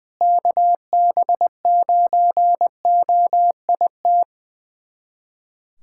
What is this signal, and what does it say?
Morse code, heard as a keyed tone, about 21 wpm, 700 Hz. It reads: KB9OIT
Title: USA Callsign DE Callsign 20wpm